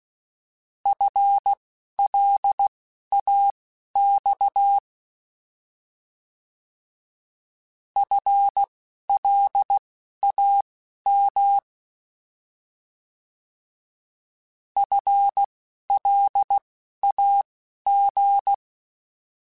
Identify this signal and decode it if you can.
Morse code, heard as a keyed tone, about 16 wpm, spaced as for 11 wpm, 800 Hz. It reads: FLAX FLAM FLAG
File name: Alphabetical / Alphabetical mp3s / Fs @10